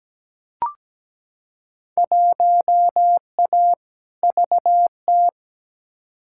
Morse code, heard as a keyed tone, about 17 wpm, 700 Hz. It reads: 1A VT
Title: Field Day Exchanges Encore - 17wpm